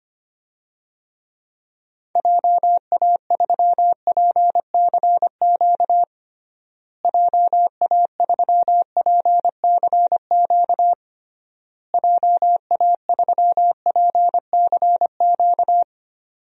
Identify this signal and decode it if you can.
Morse code, heard as a keyed tone, about 25 wpm, 700 Hz. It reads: JA3PCQ JA3PCQ JA3PCQ
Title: DXCC-100 Call Signs - 6 Characters - Repeated 3x - 25wpm